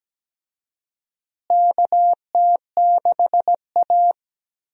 Morse code, heard as a keyed tone, about 17 wpm, 700 Hz. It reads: KT6A
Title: USA Call Signs - 4 Characters - Repeated 2x - 17wpm